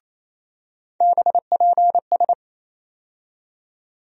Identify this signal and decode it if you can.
Morse code, heard as a keyed tone, about 28 wpm, 700 Hz. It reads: BPS
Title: USA Call Sign Postfixes - 3 Letters 28wpm